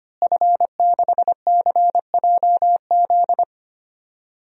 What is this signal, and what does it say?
Morse code, heard as a keyed tone, about 25 wpm, 700 Hz. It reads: F6CJZ
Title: DXCC-100 Call Signs - 5 Characters - Repeated 3x - 25wpm